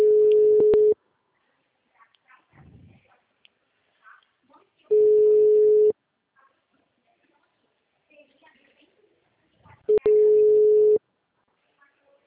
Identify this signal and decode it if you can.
Ringback tone (Europe, 425 Hz), with speech.